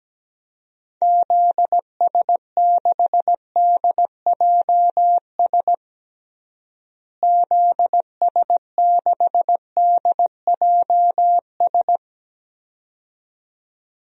Morse code, heard as a keyed tone, about 17 wpm, 700 Hz. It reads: ZS6DJS ZS6DJS
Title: DXCC-100 Call Signs - 6 Characters - Repeated 2x - 17wpm